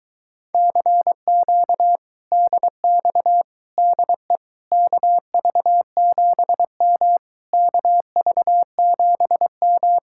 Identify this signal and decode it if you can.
Morse code, heard as a keyed tone, about 22 wpm, 700 Hz. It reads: CQ DX DE KV7M KV7M